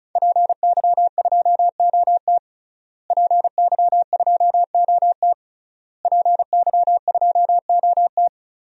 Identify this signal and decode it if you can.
Morse code, heard as a keyed tone, about 36 wpm, 700 Hz. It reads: PY2OT PY2OT PY2OT
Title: DXCC-100 Call Signs - 5 Characters - Repeated 3x - 35wpm